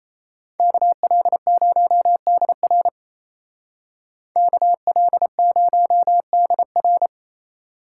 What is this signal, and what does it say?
Morse code, heard as a keyed tone, about 33 wpm, 700 Hz. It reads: KL0DR KL0DR
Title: Speed Racing - USA 2x2 Call Signs 28wpm